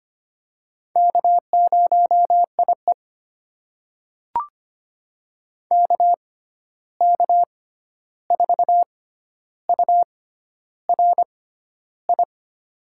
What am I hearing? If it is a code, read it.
Morse code, heard as a keyed tone, about 25 wpm, spaced as for 11 wpm, 700 Hz. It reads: K0IE KK4URI
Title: USA Call Signs - 6x Character Spacing - 25wpm